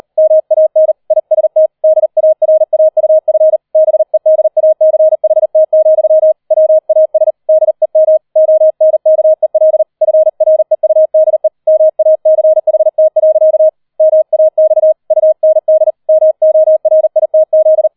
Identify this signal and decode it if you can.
Morse code, heard as a keyed tone, about 38 wpm, 630 Hz. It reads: MAN IST DARAUF BEDACHT, WAS DEM ONKEL FREUDE MACHT. MAX UND MORITZ
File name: 40wpm